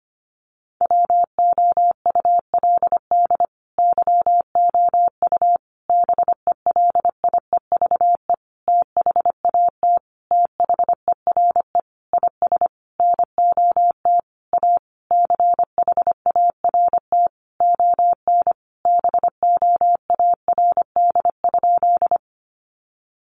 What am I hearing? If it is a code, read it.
Morse code, heard as a keyed tone, about 25 wpm, 700 Hz. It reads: WOULD YOU BELIEVE THAT THERE IS NOT A CHART ON BOARD?
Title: Sentences from Top 1300 Words 25wpm